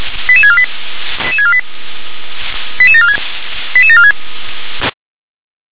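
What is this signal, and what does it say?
Five-tone selective call (ZVEI), heard as quick runs of five tones, tones of about 70 ms. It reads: 88648 88648 88746 88746
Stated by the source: ZVEI five-tone selective call received on an SDR.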